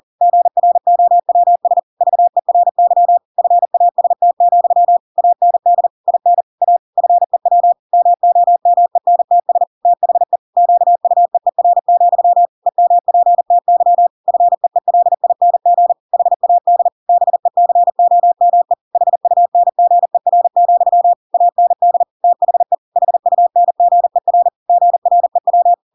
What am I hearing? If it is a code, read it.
Morse code, heard as a keyed tone, about 40 wpm, 700 Hz. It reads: GROWS VERY FAST, AND IN A FEW MOMENTS THE QUEER, EMPTY FEELING HAD BECOME HUNGER, AND THE HUNGER GREW
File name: Pinocchio / Pinocchio ch5 40 40 ik1hge